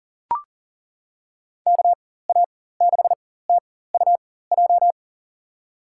Morse code, heard as a keyed tone, about 40 wpm, spaced as for 19 wpm, 700 Hz. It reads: KA6TUJ
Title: USA Call Signs - 6 Characters - 4x Character Spacing - 40wpm